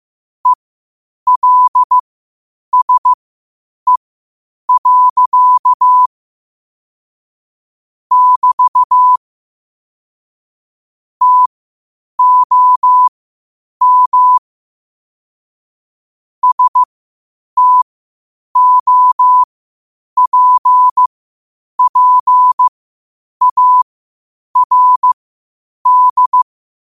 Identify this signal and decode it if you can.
Morse code, heard as a keyed tone, about 15 wpm, spaced as for 8 wpm, 1000 Hz. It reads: ELSE. <BT> TOM STOPPARD
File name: 8wpm